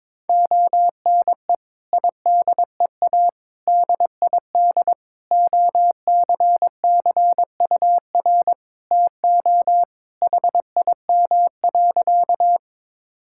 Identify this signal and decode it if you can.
Morse code, heard as a keyed tone, about 21 wpm, 700 Hz. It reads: ONE IDEA DID OCCUR TO HIM.
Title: Sentences from Top 1000 Words - Review Part 6 - 22wpm